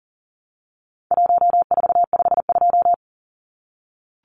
Morse code, heard as a keyed tone, about 40 wpm, 700 Hz. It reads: 1452
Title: Numbers - Contest Style - 4-Digit 40wpm